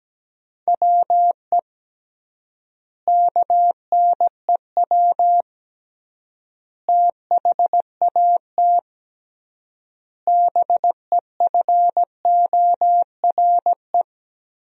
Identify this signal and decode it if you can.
Morse code, heard as a keyed tone, about 17 wpm, 700 Hz. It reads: WE KNEW THAT BEFORE
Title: 3x Word-Spacing - Sentences from Top 400 Words - 17wpm